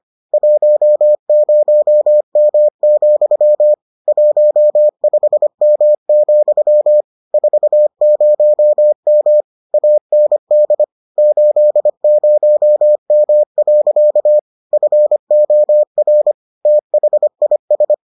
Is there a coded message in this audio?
Yes: Morse code from 0.3 to 17.9 s, heard as a keyed tone, about 25 wpm, 600 Hz. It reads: 10M, 15M, 40M AND 80M. FOR THIS